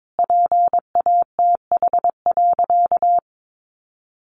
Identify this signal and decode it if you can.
Morse code, heard as a keyed tone, about 22 wpm, 700 Hz. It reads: PATH.
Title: A Sound Of Thunder 22wpm